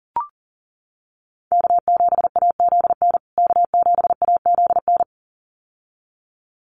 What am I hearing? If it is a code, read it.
Morse code, heard as a keyed tone, about 40 wpm, 700 Hz. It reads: K7AZN K7AZN
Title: USA Call Signs Repeated 2x 40wpm